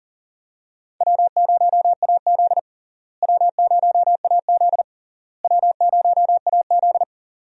Morse code, heard as a keyed tone, about 40 wpm, 700 Hz. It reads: W0AZ W0AZ W0AZ
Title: USA Call Signs - 4 Characters - Repeated 3x - 40wpm